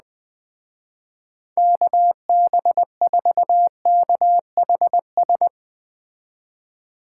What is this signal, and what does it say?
Morse code, heard as a keyed tone, about 20 wpm, 700 Hz. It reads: KB4KHS